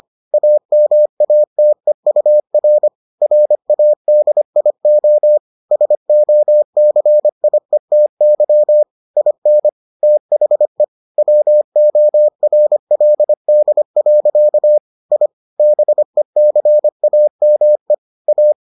Morse code, heard as a keyed tone, about 25 wpm, 600 Hz. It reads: AMATEUR RADIO SOCIETY IN THE WORLD. I BECAME A